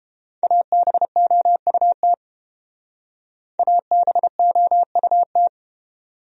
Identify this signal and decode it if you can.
Morse code, heard as a keyed tone, about 29 wpm, 700 Hz. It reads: ABOUT ABOUT